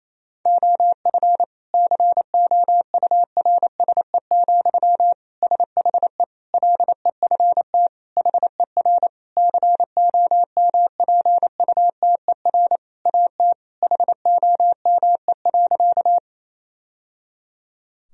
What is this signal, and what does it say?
Morse code, heard as a keyed tone, about 28 wpm, 700 Hz. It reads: OF COURSE, SHE LEFT HER COMPUTER AT HOME.
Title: Sentences from Top 1000 Words - Review Part 3 - 28wpm